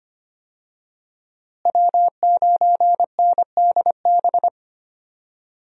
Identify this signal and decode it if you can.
Morse code, heard as a keyed tone, about 25 wpm, 700 Hz. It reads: W9NDB
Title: USA Call Signs - 5 Characters - Repeated 3x - 25wpm